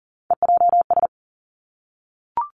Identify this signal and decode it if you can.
Morse code, heard as a keyed tone, about 40 wpm, 700 Hz. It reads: EJS
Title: Three Challenging Letter-Number Combinations 40wpm